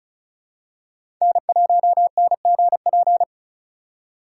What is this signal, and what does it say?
Morse code, heard as a keyed tone, about 36 wpm, 700 Hz. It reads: N1NGP
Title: USA Call Signs - 5 Characters - 6x Character Spacing - 35wpm